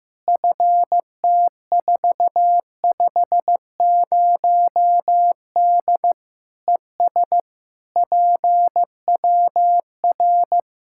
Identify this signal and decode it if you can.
Morse code, heard as a keyed tone, about 15 wpm, 700 Hz. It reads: FT450D ES PWR
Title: Practice QSO - Exchange 2 - Info - 15wpm